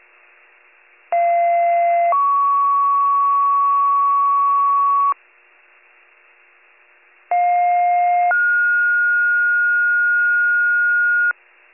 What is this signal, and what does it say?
Two-tone page, heard as a one-second tone, then a three-second tone, about 690 then 1120 Hz, about 710 then 1430 Hz. It is heard from 1.1 to 11.3 s.